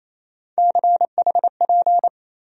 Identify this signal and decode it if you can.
Morse code, heard as a keyed tone, about 28 wpm, 700 Hz. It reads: CHP
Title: USA Call Sign Postfixes - 3 Letters 28wpm